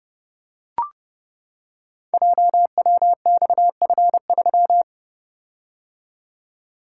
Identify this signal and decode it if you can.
Morse code, heard as a keyed tone, about 30 wpm, 700 Hz. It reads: JWXF3
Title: Code Groups - 5 Letter:Numbers - 30wpm